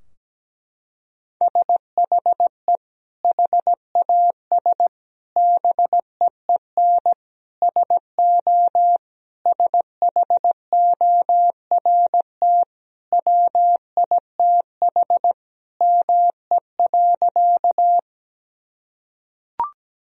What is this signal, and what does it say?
Morse code, heard as a keyed tone, about 17 wpm, 700 Hz. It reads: SHE HAS BEEN SO SHORT WITH ME.
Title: Sentences from Top 1000 Words - Review Part 2 - 17wpm